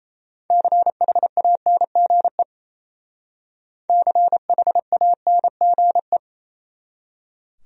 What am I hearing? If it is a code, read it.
Morse code, heard as a keyed tone, about 36 wpm, 700 Hz. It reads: CHANGE CHANGE